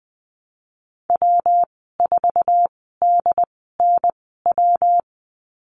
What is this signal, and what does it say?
Morse code, heard as a keyed tone, about 20 wpm, spaced as for 15 wpm, 700 Hz. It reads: W4DNW
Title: 179 Lesson 29g - Slash - Call Signs 20wpm